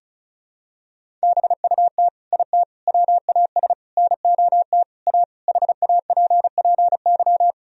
Morse code, heard as a keyed tone, about 36 wpm, 700 Hz. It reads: BUT IT WAS NOT A HAPPY
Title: Sentences from Top 1000 Words - Review Part 1 - 35wpm